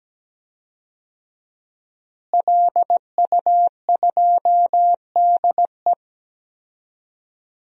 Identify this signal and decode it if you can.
Morse code, heard as a keyed tone, about 17 wpm, 700 Hz. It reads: LU2DE
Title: DXCC-100 Call Signs - 5 Characters - Repeated 3x - 17wpm